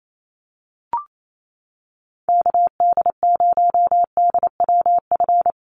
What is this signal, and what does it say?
Morse code, heard as a keyed tone, about 28 wpm, 700 Hz. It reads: KD0DWF
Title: USA Call Signs Repeated 2x 28wpm